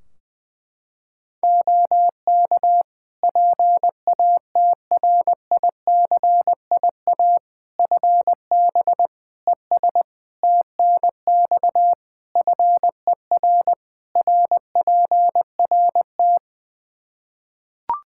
Morse code, heard as a keyed tone, about 20 wpm, 700 Hz. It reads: OK PATRICIA FB ES TNX FER RPRT
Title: Practice QSO - Exchange 2 - Info - 20wpm